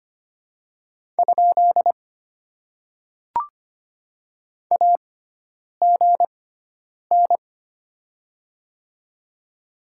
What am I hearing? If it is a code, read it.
Morse code, heard as a keyed tone, about 25 wpm, spaced as for 9 wpm, 700 Hz. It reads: ? AGN
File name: Top 90 QSO Elements - 6x Character Spacing - 25wpm